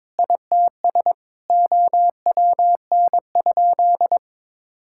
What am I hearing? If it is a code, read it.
Morse code, heard as a keyed tone, about 22 wpm, 700 Hz. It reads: ITS OWN?
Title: Sentences from Top 1000 Words - Review Part 5 - 22wpm